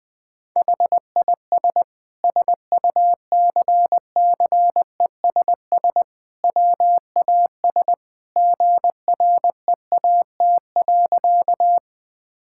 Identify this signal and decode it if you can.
Morse code, heard as a keyed tone, about 20 wpm, 700 Hz. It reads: HIS SUCCESS WAS GREAT.
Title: Sentences from Top 1000 Words - Review Part 6 - 20wpm